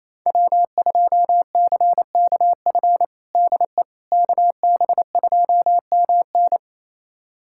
Morse code, heard as a keyed tone, about 28 wpm, 700 Hz. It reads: W2CKF DE KB2MN
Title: USA Callsign DE Callsign - 5 Characters - 28wpm